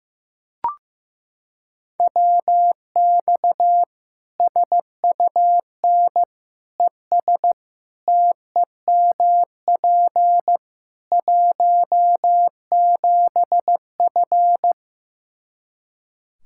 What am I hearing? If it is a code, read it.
Morse code, heard as a keyed tone, about 15 wpm, 700 Hz. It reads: WX SUN ES TEMP 17F
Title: Practice QSO - Exchange 2 - Info - 15wpm